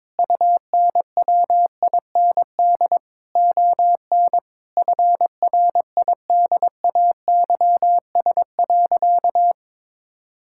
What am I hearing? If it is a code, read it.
Morse code, heard as a keyed tone, about 22 wpm, 700 Hz. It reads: UNWIND ON FRIDAYS.
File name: Difficult Sentences with Hints - 22wpm